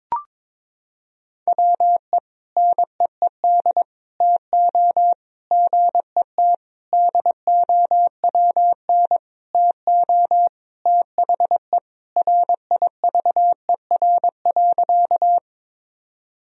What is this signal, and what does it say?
Morse code, heard as a keyed tone, about 22 wpm, 700 Hz. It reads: WE NEED TO GET DOWN TO THE RIVER.
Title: Sentences from Top 1000 Words - Review Part 1 - 22wpm